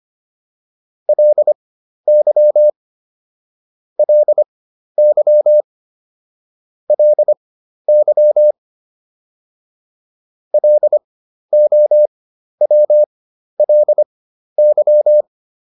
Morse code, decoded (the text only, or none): LY LY LY LOWLY